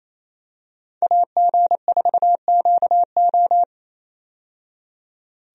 Morse code, heard as a keyed tone, about 28 wpm, 700 Hz. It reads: AG4QO
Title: USA Call Signs - 5 Characters - 28wpm